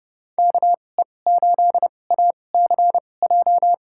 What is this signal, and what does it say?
Morse code, heard as a keyed tone, about 30 wpm, spaced as for 22 wpm, 700 Hz. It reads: KE8ACJ
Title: USA Call Signs - 6 Characters - 2x Character Spacing - 30wpm